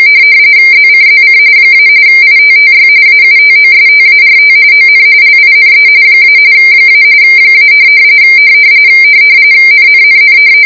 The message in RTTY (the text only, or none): ZCZC - ITTY AUTOSTART ON LINE - - - NNNN